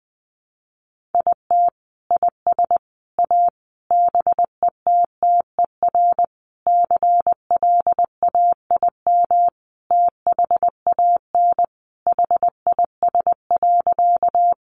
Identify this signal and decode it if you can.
Morse code, heard as a keyed tone, about 20 wpm, 700 Hz. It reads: IT IS A BETTER CLAIM THAN HIS.
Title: Sentences from Top 1500 Words 20wpm